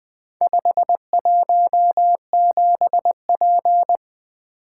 Morse code, heard as a keyed tone, about 20 wpm, 700 Hz. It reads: 517P
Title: Four Letter-Number Combinations - 20wpm